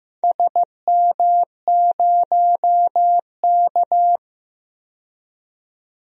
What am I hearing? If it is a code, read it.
Morse code, heard as a keyed tone, about 16 wpm, 700 Hz. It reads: SM0K
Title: DXCC-100 Call Signs - 4 Characters - 15wpm